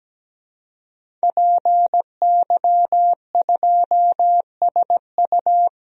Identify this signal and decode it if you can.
Morse code, heard as a keyed tone, about 17 wpm, 700 Hz. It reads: PY2SU
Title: DXCC-100 Call Signs - 5 Characters - Repeated 2x - 17wpm